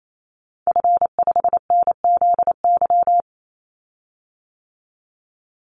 Morse code, heard as a keyed tone, about 28 wpm, 700 Hz. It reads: F5NZY
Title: Call Signs - 28wpm